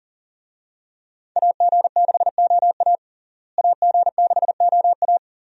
Morse code, heard as a keyed tone, about 37 wpm, 700 Hz. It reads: AG6OA AG6OA
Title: USA Call Signs - 5 Characters - Repeated 2x - 40wpm